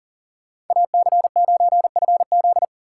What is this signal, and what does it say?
Morse code, heard as a keyed tone, about 52 wpm, 700 Hz. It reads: AC9FZ